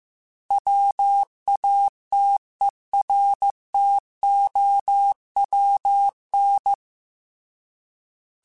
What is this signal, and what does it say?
Morse code, heard as a keyed tone, about 15 wpm, 800 Hz. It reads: WATERTOWN